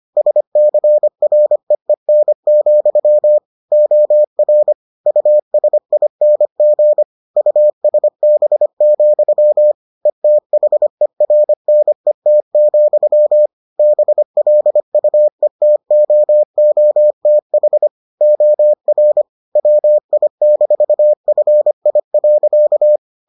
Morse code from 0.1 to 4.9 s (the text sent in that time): SCREEN, OR